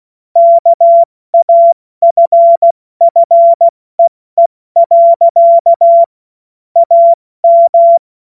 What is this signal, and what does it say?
Morse code, heard as a keyed tone, about 16 wpm, spaced as for 14 wpm, 680 Hz. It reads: KAFFEE. AM
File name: Morsekurs / Witze / Witz 25. Mai 2021, 70 F80